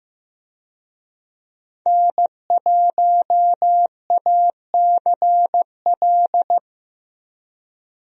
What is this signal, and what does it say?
Morse code, heard as a keyed tone, about 15 wpm, 700 Hz. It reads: N1ACL
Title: USA Call Signs - 5 Characters - 15wpm